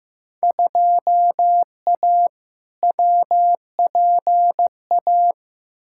Morse code, heard as a keyed tone, about 15 wpm, 700 Hz. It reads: 2A WPA
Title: Field Day Exchanges - 15wpm